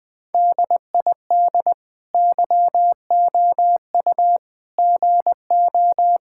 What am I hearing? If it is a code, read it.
Morse code, heard as a keyed tone, about 20 wpm, 700 Hz. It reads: DID YOU GO
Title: Sentences from Top 1000 Words - Review Part 5 - 20wpm